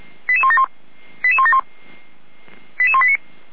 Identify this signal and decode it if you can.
Five-tone selective call (ZVEI), heard as quick runs of five tones, tones of about 70 ms. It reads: 88171 88171 88189